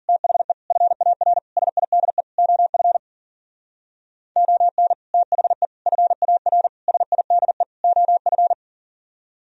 Morse code, heard as a keyed tone, about 50 wpm, 700 Hz. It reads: THE FAR SIDE OF ON THE FAR SIDE OF
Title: Speed-Racing - Sets of 5 Words 40wpm